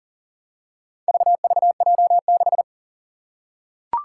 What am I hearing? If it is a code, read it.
Morse code, heard as a keyed tone, about 40 wpm, 700 Hz. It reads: VVJ6
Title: Four Challenging Letter-Number Combinations 40wpm